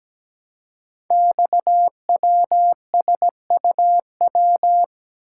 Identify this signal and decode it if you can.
Morse code, heard as a keyed tone, about 17 wpm, 700 Hz. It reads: XWSUW
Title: Code Groups - 5 Letters - 17wpm